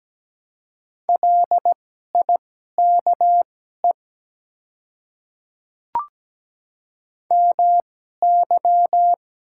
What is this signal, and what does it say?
Morse code, heard as a keyed tone, about 17 wpm, spaced as for 12 wpm, 700 Hz. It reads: LIKE MY
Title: 2x Character Spacing - Top 100 Words - 17wpm